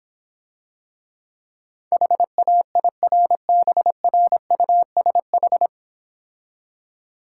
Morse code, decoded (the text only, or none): HAIRBRUSH